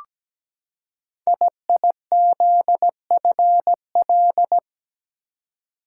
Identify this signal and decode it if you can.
Morse code, heard as a keyed tone, about 17 wpm, 700 Hz. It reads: IIZFL
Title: Code Groups - 5 Letters - 17wpm